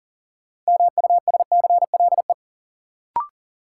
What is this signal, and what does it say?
Morse code, heard as a keyed tone, about 40 wpm, 700 Hz. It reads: MUSCLE
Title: Sentences from Top 1300 Words - New Word Drill Alt - 40wpm